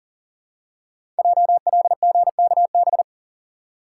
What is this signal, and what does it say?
Morse code, heard as a keyed tone, about 40 wpm, 700 Hz. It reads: JLGKB
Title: Code Groups - 5 Letters - 40wpm